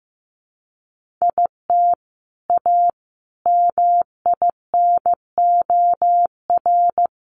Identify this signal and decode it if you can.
Morse code, heard as a keyed tone, about 15 wpm, 700 Hz. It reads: IT A MINOR
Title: Sentences from Top 1500 Words 15wpm